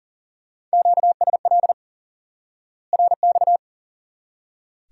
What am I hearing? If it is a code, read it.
Morse code, heard as a keyed tone, about 40 wpm, 700 Hz. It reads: QSL RX
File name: IQE Course 3 - Lesson 89 - YAGI - Two Elements - 40wpm